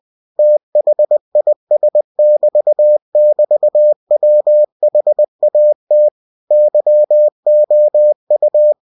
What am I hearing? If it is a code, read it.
Morse code, heard as a keyed tone, about 20 wpm, 600 Hz. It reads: THIS<BT><BT>WHAT YOU
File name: cw-books / winnie-the-pooh / winnie-20wpm-page-0030